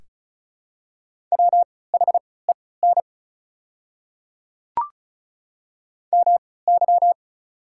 Morse code, heard as a keyed tone, about 36 wpm, spaced as for 20 wpm, 690 Hz. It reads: WHEN MY
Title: 3x Character Spacing - Top 100 Words - 35wpm